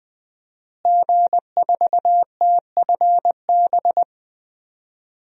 Morse code, heard as a keyed tone, about 20 wpm, 700 Hz. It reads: G4TFB
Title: DXCC-100 Call Signs - 5 Characters - Repeated 3x - 20wpm